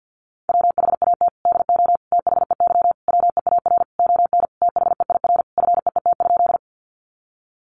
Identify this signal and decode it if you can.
Morse code, heard as a keyed tone, about 50 wpm, 700 Hz. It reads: WHAT DO THEY WEAR ON THEIR FEET?
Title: Sentences from Top 1500 Words 50wpm